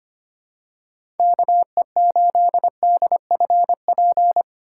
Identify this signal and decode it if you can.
Morse code, heard as a keyed tone, about 25 wpm, 700 Hz. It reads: KE8DFP
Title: USA Call Signs - 6 Characters - 3x Character Spacing - 25wpm